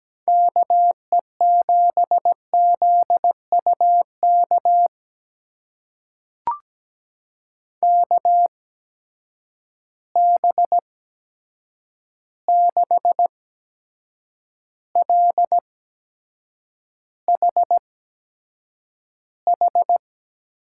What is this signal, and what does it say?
Morse code, heard as a keyed tone, about 17 wpm, spaced as for 7 wpm, 700 Hz. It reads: KE7ZUK KB6LHH
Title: USA Call Signs - 6 Characters - 8x Character Spacing - 17wpm